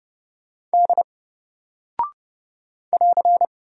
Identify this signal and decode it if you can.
Morse code, heard as a keyed tone, about 30 wpm, 700 Hz. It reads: D <AR>
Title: Single Character (Letter:Number:Punctuation:Prosign) Alt 30wpm